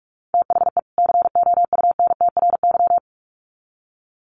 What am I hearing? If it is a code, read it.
Morse code, heard as a keyed tone, about 44 wpm, 700 Hz. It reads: THE COUNTRY
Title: Sets of 2 Words 45wpm